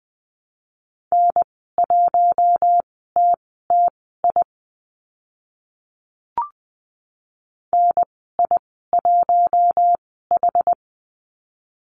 Morse code, heard as a keyed tone, about 20 wpm, spaced as for 14 wpm, 700 Hz. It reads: N1TTI NI1H
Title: 062 Lesson 11g - Letter h - Call Signs 20wpm